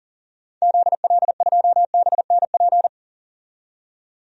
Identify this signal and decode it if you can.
Morse code, heard as a keyed tone, about 38 wpm, 700 Hz. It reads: ZL2BNP
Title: DXCC-100 Call Signs - 6 Characters - 40wpm